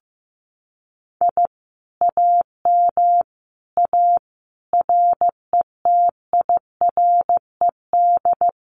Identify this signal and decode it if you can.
Morse code, heard as a keyed tone, about 15 wpm, 700 Hz. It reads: I AM A RETIRED
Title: Common QSO Phrases Encore 15wpm